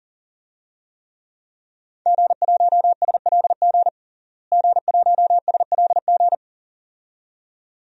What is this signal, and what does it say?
Morse code, heard as a keyed tone, about 37 wpm, 700 Hz. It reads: G1SLG G1SLG
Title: DXCC-100 Call Signs - 5 Characters - Repeated 2x - 40wpm